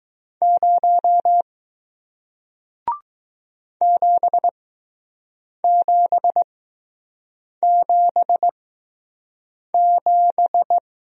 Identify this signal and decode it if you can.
Morse code, heard as a keyed tone, about 20 wpm, spaced as for 6 wpm, 700 Hz. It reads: 0 7777